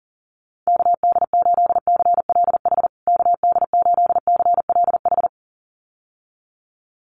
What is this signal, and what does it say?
Morse code, heard as a keyed tone, about 40 wpm, 700 Hz. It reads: KD8CLH KD8CLH
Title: USA Call Signs Repeated 2x 40wpm